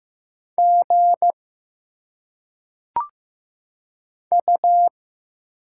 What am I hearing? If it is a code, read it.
Morse code, heard as a keyed tone, about 15 wpm, 700 Hz. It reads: G U
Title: Single Letters - Phonetic Pronunciation 15wpm